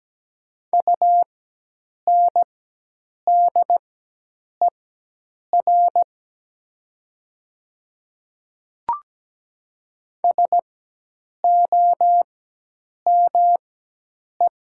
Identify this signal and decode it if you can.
Morse code, heard as a keyed tone, about 17 wpm, spaced as for 8 wpm, 700 Hz. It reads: UNDER SOME
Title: 4x Character Spacing - Top 300 Words - 17wpm